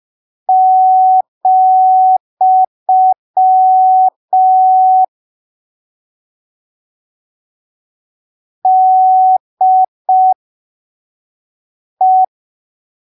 Morse code, heard as a keyed tone, about 5 wpm, spaced as for 3 wpm, 750 Hz. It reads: , DE
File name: arquivos / texto / texto 18 05ppm